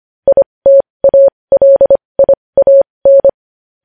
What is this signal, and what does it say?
Morse code, heard as a keyed tone, about 25 wpm, spaced as for 20 wpm, 550 Hz. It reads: ITALIAN
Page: news-headlines-in-morse-code-at-20-wpm